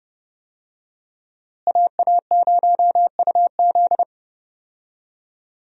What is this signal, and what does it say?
Morse code, heard as a keyed tone, about 30 wpm, 700 Hz. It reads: AA0UZ
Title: USA Call Signs - 5 Characters - Repeated 2x - 30wpm